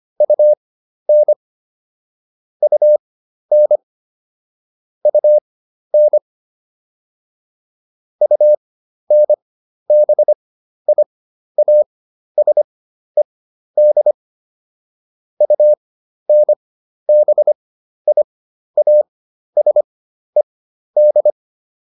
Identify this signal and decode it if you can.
Morse code, heard as a keyed tone, about 25 wpm, spaced as for 12 wpm, 600 Hz. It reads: UN UN UN UNBIASED UNBIASED